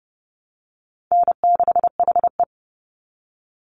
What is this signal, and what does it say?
Morse code, heard as a keyed tone, about 30 wpm, 700 Hz. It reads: N6HE
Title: Easy Call Signs Repeated 3x - 30wpm